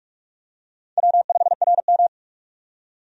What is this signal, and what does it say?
Morse code, heard as a keyed tone, about 46 wpm, 700 Hz. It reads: W5RM
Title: USA Call Signs - 4 Characters - 45wpm